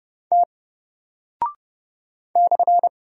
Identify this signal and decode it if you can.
Morse code, heard as a keyed tone, about 30 wpm, 700 Hz. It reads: T /
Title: Single Character (Letter:Number:Punctuation:Prosign) Alt 30wpm